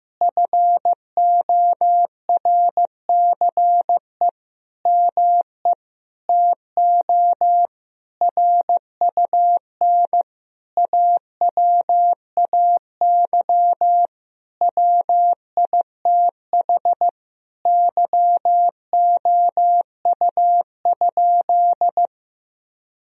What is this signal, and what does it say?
Morse code, heard as a keyed tone, about 15 wpm, 700 Hz. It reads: FORCE ME TO RUN AWAY WITH YOU?
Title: Sentences from Top 1000 Words - Review Part 3 - 15wpm